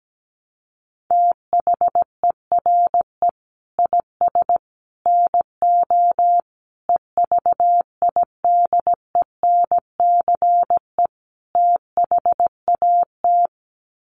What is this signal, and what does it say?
Morse code, heard as a keyed tone, about 17 wpm, 700 Hz. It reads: THERE IS NO EVIDENCE THAT
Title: Sets of 5 Words Encore 17wpm